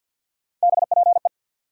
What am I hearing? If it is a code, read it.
Morse code, heard as a keyed tone, about 47 wpm, 700 Hz. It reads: BPE
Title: USA Call Sign Postfixes - 3 Letters 50wpm